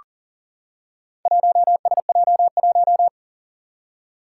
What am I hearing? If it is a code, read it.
Morse code, heard as a keyed tone, about 37 wpm, 700 Hz. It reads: 1SJ1
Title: Four Challenging Letter-Number Combinations 40wpm